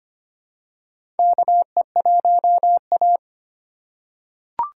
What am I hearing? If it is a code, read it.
Morse code, heard as a keyed tone, about 25 wpm, 700 Hz. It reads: KE1A
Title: USA Call Signs - 4 Characters - 4x Character Spacing - 25wpm